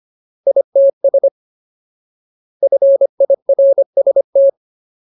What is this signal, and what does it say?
Morse code, heard as a keyed tone, about 25 wpm, 550 Hz. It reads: ITS FIRST
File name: Ultimate Word List / 25WPM Ultimate